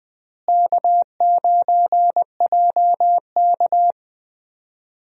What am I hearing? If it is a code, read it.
Morse code, heard as a keyed tone, about 20 wpm, 700 Hz. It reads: K9JK
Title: USA Call Signs - 4 Characters - Repeated 2x - 20wpm